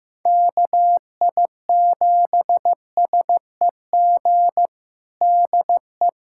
Morse code, heard as a keyed tone, about 15 wpm, 700 Hz. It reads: KI7SEG DE
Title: USA Callsign DE Callsign - 6 Characters - 15wpm